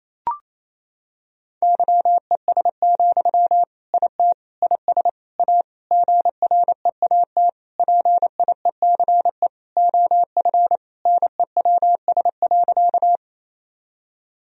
Morse code, heard as a keyed tone, about 28 wpm, 700 Hz. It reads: YES, IT IS A GREAT PIECE OF NEWS.
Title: Sentences from Top 1000 Words - Review Part 2 - 28wpm